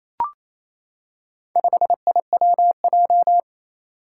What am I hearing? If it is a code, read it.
Morse code, heard as a keyed tone, about 27 wpm, 700 Hz. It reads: 5IWJ
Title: Four Challenging Letter-Number Combinations 28wpm